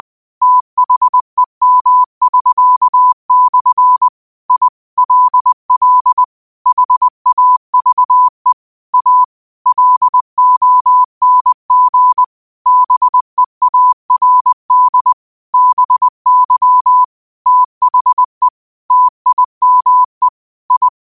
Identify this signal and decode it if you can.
Morse code, heard as a keyed tone, about 20 wpm, 1000 Hz. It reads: THEM<SK>/ I LL HAVE A LONG BEARD BY THE TIME I
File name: qotdfiles / 20wpm